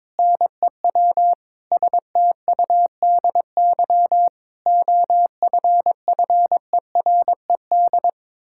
Morse code, heard as a keyed tone, about 22 wpm, 700 Hz. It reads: NEW STUDY OFFERED
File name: practice / 22wpm / Sentences with 8-Word Phrases - 22wpm